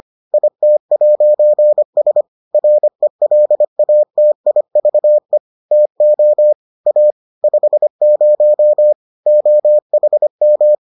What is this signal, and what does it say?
Morse code, heard as a keyed tone, about 25 wpm, 600 Hz. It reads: IT'S RELATIVE TO A 50 OHM